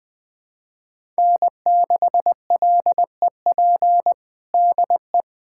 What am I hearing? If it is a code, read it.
Morse code, heard as a keyed tone, about 20 wpm, 700 Hz. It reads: N6LEP DE
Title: USA Callsign DE Callsign - 5 Characters - 20wpm